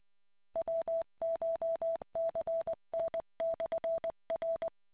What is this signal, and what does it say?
Morse code, heard as a keyed tone, about 20 wpm, 660 Hz. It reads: WCHCN/R